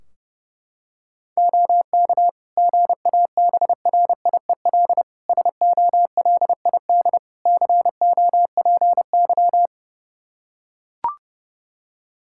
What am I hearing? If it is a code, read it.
Morse code, heard as a keyed tone, about 30 wpm, 700 Hz. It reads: OK GABRIEL SOLID COPY
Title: Practice QSO - Exchange 3 - More Info - 30wpm